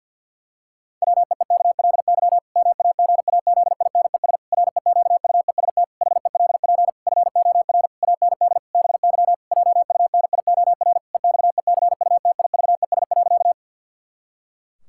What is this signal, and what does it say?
Morse code, heard as a keyed tone, about 48 wpm, 700 Hz. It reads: WEEKLY MAGAZINES REQUEST HELP FOR AND BY JUNIOR EXECUTIVES.
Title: Pangram Challenge v2 - 50wpm